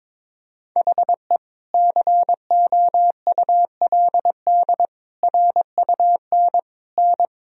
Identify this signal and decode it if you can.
Morse code, heard as a keyed tone, about 22 wpm, 700 Hz. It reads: HE COULD RUN N
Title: Sentences from Top 300 Words and Abbreviations 22wpm